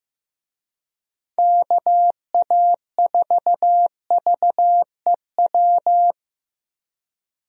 Morse code, heard as a keyed tone, about 16 wpm, 700 Hz. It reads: KA4VEW
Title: USA Call Signs - 6 Characters - 15wpm